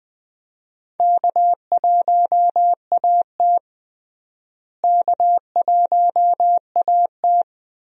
Morse code, heard as a keyed tone, about 20 wpm, 700 Hz. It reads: K1AT K1AT
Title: USA Call Signs - 4 Characters - Repeated 3x - 20wpm